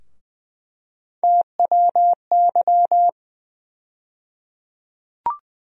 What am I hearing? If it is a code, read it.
Morse code, heard as a keyed tone, about 20 wpm, 700 Hz. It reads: TWY
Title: USA Call Sign Postfixes - 3 Letters 20wpm